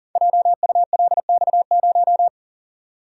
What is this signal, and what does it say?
Morse code, heard as a keyed tone, about 40 wpm, 700 Hz. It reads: JULX0
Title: Code Groups - 5 Letter:Numbers - 40wpm